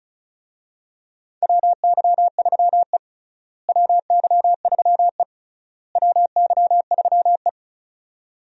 Morse code, heard as a keyed tone, about 36 wpm, 700 Hz. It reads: WY3E WY3E WY3E
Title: USA Call Signs - 4 Characters - Repeated 3x - 35wpm